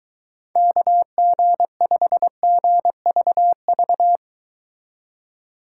Morse code, heard as a keyed tone, about 23 wpm, 700 Hz. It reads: KG5GVV